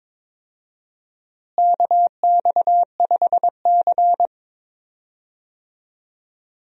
Morse code, heard as a keyed tone, about 22 wpm, 700 Hz. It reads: KX5C
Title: USA Call Signs - 4 Characters - Repeated 2x - 22wpm